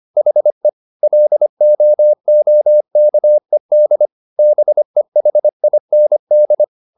Morse code, heard as a keyed tone, about 25 wpm, 600 Hz. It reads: HE LOOKED BEHIND
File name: the old man and the sea part 1 25wpm